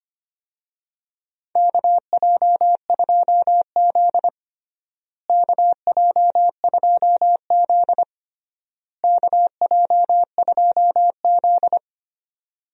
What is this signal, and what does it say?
Morse code, heard as a keyed tone, about 25 wpm, 700 Hz. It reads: KJ2Z KJ2Z KJ2Z
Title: USA Call Signs - 4 Characters - Repeated 3x - 25wpm